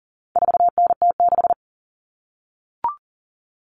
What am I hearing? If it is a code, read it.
Morse code, heard as a keyed tone, about 40 wpm, 700 Hz. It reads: 4NT6
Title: Common Cut Numbers - 4-Digit 40wpm